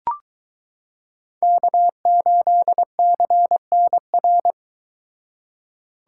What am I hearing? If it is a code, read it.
Morse code, heard as a keyed tone, about 23 wpm, 700 Hz. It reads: K8CNR